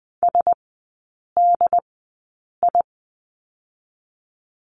Morse code, heard as a keyed tone, about 20 wpm, spaced as for 8 wpm, 700 Hz. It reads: SDI
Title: 067 Lesson 12e - Letter d - Three Characters 20wpm